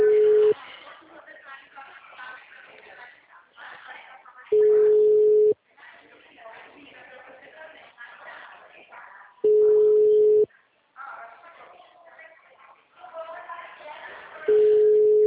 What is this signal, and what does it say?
Ringback tone (Europe, 425 Hz), with speech.
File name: out- -2013- - - .21244